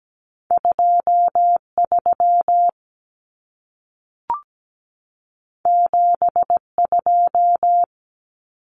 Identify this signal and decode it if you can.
Morse code, heard as a keyed tone, about 17 wpm, 700 Hz. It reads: 23 72
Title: Common Cut Numbers - 2-Digit 17wpm